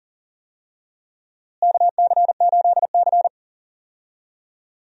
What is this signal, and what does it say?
Morse code, heard as a keyed tone, about 37 wpm, 700 Hz. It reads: KC8C
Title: USA Call Signs - 4 Characters - Repeated 3x - 40wpm